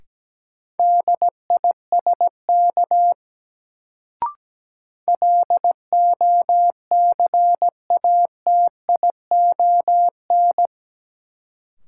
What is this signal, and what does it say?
Morse code, heard as a keyed tone, about 17 wpm, 700 Hz. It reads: DISK LOCATION
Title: Sentences from Top 1000 Words - Review Part 6 - New Word Drill - 17wpm